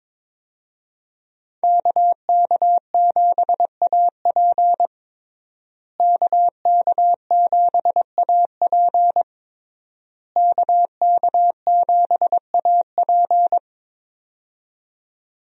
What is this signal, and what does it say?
Morse code, heard as a keyed tone, about 22 wpm, 700 Hz. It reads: KK7AP KK7AP KK7AP
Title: USA Call Signs - 5 Characters - Repeated 3x - 22wpm